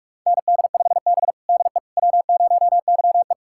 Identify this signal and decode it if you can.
Morse code, heard as a keyed tone, about 41 wpm, 700 Hz. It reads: ND5B DE W0YE
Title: USA Callsign DE Callsign - 4 Characters - 45wpm